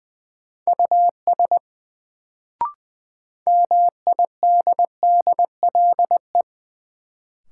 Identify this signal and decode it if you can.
Morse code, heard as a keyed tone, about 20 wpm, 700 Hz. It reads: US MIDDLE
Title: Sets of 2 Words - New Word Drill - 20wpm